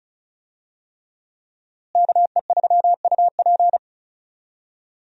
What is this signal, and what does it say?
Morse code, heard as a keyed tone, about 36 wpm, 700 Hz. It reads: KE3UP
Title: USA Call Signs - 5 Characters - Repeated 3x - 35wpm